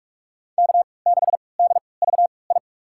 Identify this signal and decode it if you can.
Morse code, heard as a keyed tone, about 42 wpm, spaced as for 25 wpm, 700 Hz. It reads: K6DVI
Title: USA Call Signs - 3x Character Spacing - 45wpm